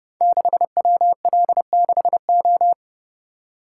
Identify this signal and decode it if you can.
Morse code, heard as a keyed tone, about 30 wpm, 700 Hz. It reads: 6WL6O
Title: Code Groups - 5 Letter:Numbers - 30wpm